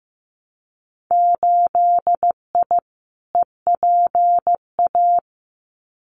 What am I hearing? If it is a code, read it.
Morse code, heard as a keyed tone, about 15 wpm, 700 Hz. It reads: 8I EPA
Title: Winter Field Day Exchanges Encore 15wpm